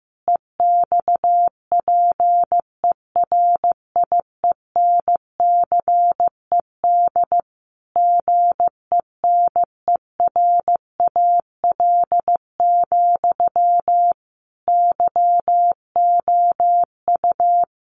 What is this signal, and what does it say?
Morse code, heard as a keyed tone, about 15 wpm, 700 Hz. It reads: EXPERIENCED GENERAL, YOU
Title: Sentences from Top 1500 Words 15wpm